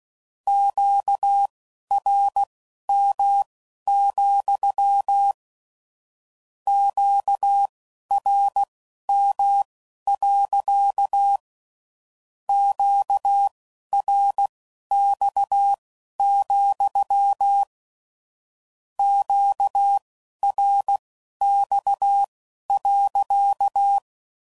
Morse code, heard as a keyed tone, about 16 wpm, spaced as for 11 wpm, 800 Hz. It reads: QRM, QRM. QRX, QRX.
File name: CW Code Groups @15WPM / Q Calls / Q Calls sppc